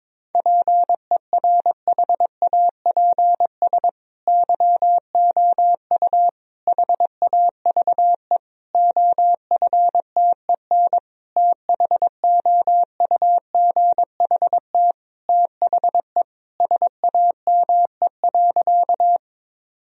Morse code, heard as a keyed tone, about 22 wpm, 700 Hz. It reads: PERHAPS YOU HAVE OFTEN THOUGHT THE SAME.
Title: Sentences from Top 1000 Words - Review Part 1 - 22wpm